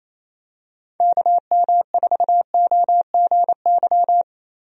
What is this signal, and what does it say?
Morse code, heard as a keyed tone, about 27 wpm, 700 Hz. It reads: KM4OGY
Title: USA Call Signs - 6 Characters - Repeated 3x - 28wpm